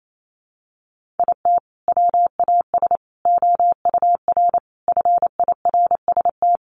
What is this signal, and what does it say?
Morse code, heard as a keyed tone, about 28 wpm, 700 Hz. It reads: IT WAS OUR FIRST
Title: Sentences from Top 1500 Words 28wpm